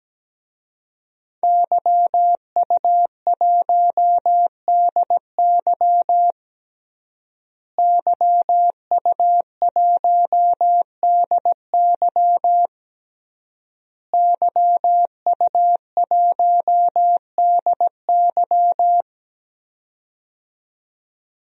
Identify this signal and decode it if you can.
Morse code, heard as a keyed tone, about 17 wpm, 700 Hz. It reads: YU1DY YU1DY YU1DY
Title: DXCC-100 Call Signs - 5 Characters - Repeated 3x - 17wpm